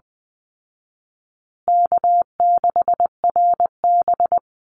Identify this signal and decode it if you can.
Morse code, heard as a keyed tone, about 20 wpm, 700 Hz. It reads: K6RB
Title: Easy Call Signs Repeated 3x - 20wpm